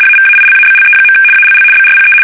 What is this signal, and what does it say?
RTTY, about 75 baud. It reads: AVAILABLE. ALL BRPE
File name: fsk850-75B / teletype-corp